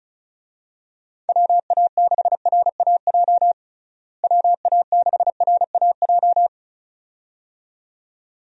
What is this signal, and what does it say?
Morse code, heard as a keyed tone, about 36 wpm, 700 Hz. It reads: WA6RAJ WA6RAJ
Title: USA Call Signs - 6 Characters - Repeated 2x - 35wpm